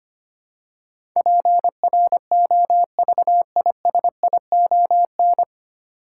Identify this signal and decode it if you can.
Morse code, heard as a keyed tone, about 25 wpm, 700 Hz. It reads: PROVISION
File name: Nine Letter Words - 25wpm